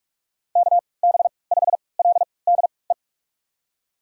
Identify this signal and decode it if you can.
Morse code, heard as a keyed tone, about 46 wpm, spaced as for 25 wpm, 700 Hz. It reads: KB5LDE
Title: USA Call Signs - 6 Characters - 3x Character Spacing - 45wpm